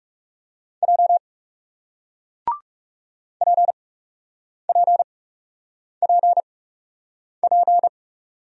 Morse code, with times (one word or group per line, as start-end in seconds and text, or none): 0.8-1.2 J
3.4-7.9 PPPP